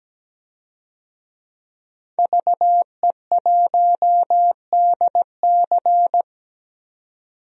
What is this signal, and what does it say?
Morse code, heard as a keyed tone, about 17 wpm, 700 Hz. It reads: VE1DC
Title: DXCC-100 Call Signs - 5 Characters - 17wpm